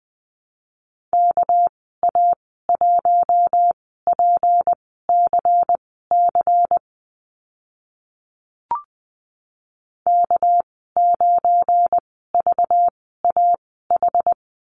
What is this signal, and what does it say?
Morse code, heard as a keyed tone, about 20 wpm, spaced as for 14 wpm, 700 Hz. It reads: KA1PCC K9VAH
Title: 213 Lesson 34g - Number 8 - Call Signs 20wpm